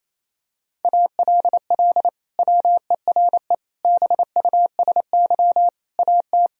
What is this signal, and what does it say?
Morse code, heard as a keyed tone, about 27 wpm, 700 Hz. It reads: ALL WERE BUSY AT
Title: Sentences from Top 1000 Words - Review Part 3 - 28wpm